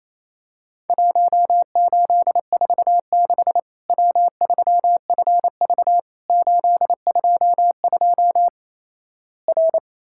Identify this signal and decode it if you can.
Morse code, heard as a keyed tone, about 28 wpm, 700 Hz. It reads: 1846 W3FV 822 R
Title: Worked All Europe DX Contest - QTC Exchange - 28wpm